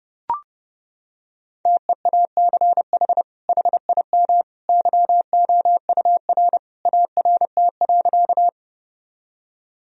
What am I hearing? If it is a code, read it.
Morse code, heard as a keyed tone, about 32 wpm, 700 Hz. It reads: TEACH HIM YOUR ART.
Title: Sentences from Top 1000 Words - Review Part 3 - 30wpm